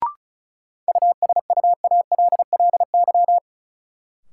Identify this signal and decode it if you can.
Morse code, heard as a keyed tone, about 36 wpm, 700 Hz. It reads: USUALLY
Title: Sentences from Top 1000 Words - Review Part 3 - New Word Drill - 35wpm